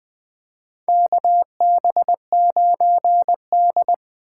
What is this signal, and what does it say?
Morse code, heard as a keyed tone, about 20 wpm, 700 Hz. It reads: KB9D
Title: USA Call Signs - 4x Character Spacing - 20wpm